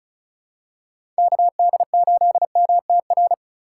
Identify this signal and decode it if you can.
Morse code, heard as a keyed tone, about 34 wpm, 700 Hz. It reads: KD8MTR
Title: USA Call Signs - 6 Characters - 2x Character Spacing - 35wpm